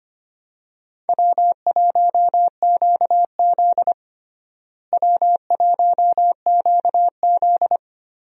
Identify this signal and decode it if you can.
Morse code, heard as a keyed tone, about 25 wpm, 700 Hz. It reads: W1QZ W1QZ
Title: USA Call Signs - 4 Characters - Repeated 2x - 25wpm